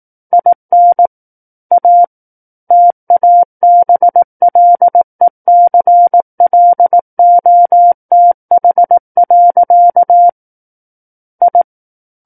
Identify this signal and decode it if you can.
Morse code, heard as a keyed tone, about 18 wpm, 700 Hz. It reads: IN A TABLECLOTH. I
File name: Book One / 09-Chapter 9 - 18 WPM